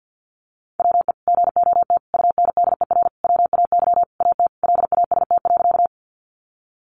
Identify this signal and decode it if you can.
Morse code, heard as a keyed tone, about 50 wpm, 700 Hz. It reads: WE GOT UNDER WAY AT LAST.
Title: Sentences from Top 300 Words 50wpm